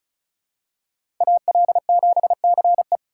Morse code, heard as a keyed tone, about 34 wpm, 700 Hz. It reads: AL7CE
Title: USA Call Signs - 2x Character Spacing - 35wpm